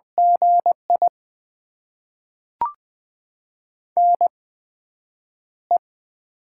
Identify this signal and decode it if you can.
Morse code, heard as a keyed tone, about 20 wpm, spaced as for 9 wpm, 700 Hz. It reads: GI NE
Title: USA Call Signs - 4 Characters - 8x Character Spacing - 20wpm